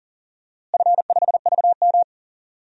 Morse code, heard as a keyed tone, about 38 wpm, 700 Hz. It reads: F5VM
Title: DXCC-100 Call Signs - 4 Characters - Repeated 3x - 40wpm